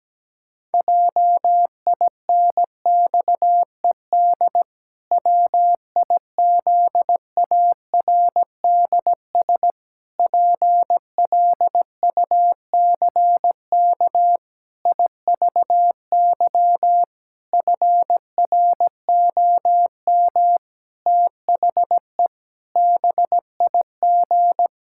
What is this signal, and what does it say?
Morse code, heard as a keyed tone, about 17 wpm, 700 Hz. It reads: JINXED WIZARDS PLUCK IVY FROM THE BIG
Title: Pangram Challenge v2 - 17wpm